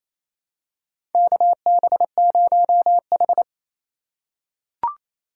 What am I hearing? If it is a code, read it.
Morse code, heard as a keyed tone, about 28 wpm, 700 Hz. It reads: KB0H
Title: USA Call Signs - 4 Characters - 2x Character Spacing - 28wpm